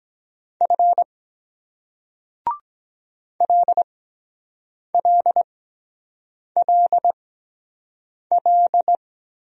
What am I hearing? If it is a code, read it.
Morse code, heard as a keyed tone, about 23 wpm, spaced as for 7 wpm, 700 Hz. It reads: F LLLL